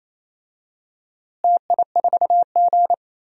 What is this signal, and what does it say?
Morse code, heard as a keyed tone, about 28 wpm, 700 Hz. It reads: TI4G
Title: DXCC-100 Call Signs - 4 Characters - Repeated 3x - 28wpm